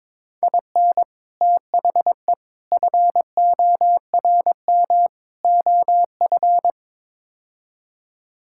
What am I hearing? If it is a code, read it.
Morse code, heard as a keyed tone, about 22 wpm, 700 Hz. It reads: IN THE FORM OF
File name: practice / 22wpm / Build A Phrase - 5-Word Phrases - 22wpm